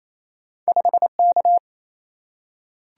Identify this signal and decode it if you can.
Morse code, heard as a keyed tone, about 28 wpm, 700 Hz. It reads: 5 K
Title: Single-Letter Single-Number 28wpm